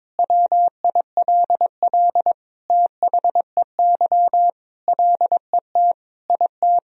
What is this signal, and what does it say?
Morse code, heard as a keyed tone, about 22 wpm, 700 Hz. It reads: WILL THEY LET IT
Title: Sentences from Top 400 Words and Abbreviations 22wpm